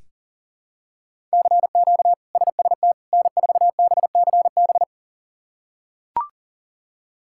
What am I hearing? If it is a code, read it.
Morse code, heard as a keyed tone, about 40 wpm, 700 Hz. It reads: CQ SST N4BCB
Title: K1USN SST QSO Practice - 40wpm